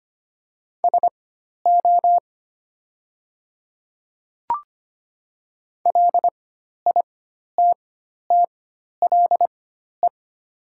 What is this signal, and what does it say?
Morse code, heard as a keyed tone, about 25 wpm, spaced as for 12 wpm, 700 Hz. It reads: SO LITTLE
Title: 4x Character Spacing - Top 200 Words - 25wpm